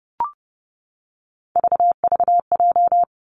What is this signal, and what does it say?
Morse code, heard as a keyed tone, about 30 wpm, 700 Hz. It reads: VVJ
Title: Three Challenging Letter-Number Combinations 30wpm